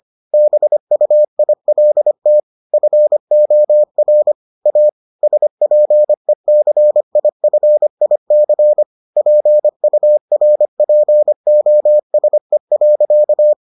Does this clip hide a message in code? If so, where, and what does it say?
Yes: Morse code from 0.3 to 13.5 s, heard as a keyed tone, about 25 wpm, 600 Hz. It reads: BUILT FOR A SPECIFIC PURPOSE.